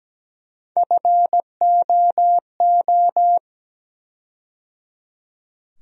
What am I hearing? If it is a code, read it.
Morse code, heard as a keyed tone, about 17 wpm, 700 Hz. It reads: FOO
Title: USA Call Sign Postfixes - 3 Letters 17wpm